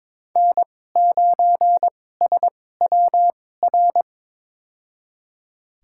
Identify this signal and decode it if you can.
Morse code, heard as a keyed tone, about 22 wpm, spaced as for 16 wpm, 700 Hz. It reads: N9SWR
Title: USA Call Signs - 5 Characters - 2x Character Spacing - 22wpm